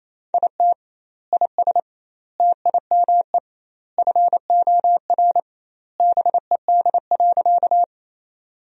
Morse code, heard as a keyed tone, about 28 wpm, 700 Hz. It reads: IT IS TIME FOR BED.
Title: 2x Word-Spacing - Sentences from Top 500 Words - 28wpm